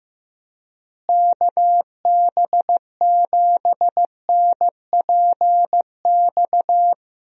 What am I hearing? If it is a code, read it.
Morse code, heard as a keyed tone, about 15 wpm, 700 Hz. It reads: KB7NPX
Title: USA Call Signs - 6 Characters - Repeated 3x - 15wpm